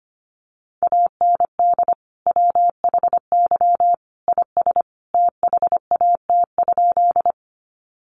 Morse code, heard as a keyed tone, about 25 wpm, 700 Hz. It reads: AND WHY IS THAT?
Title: Sentences from Top 200 Words 25wpm